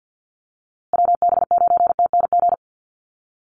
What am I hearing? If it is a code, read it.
Morse code, heard as a keyed tone, about 49 wpm, 700 Hz. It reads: WB9TNG
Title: USA Call Signs Repeated 3x 50wpm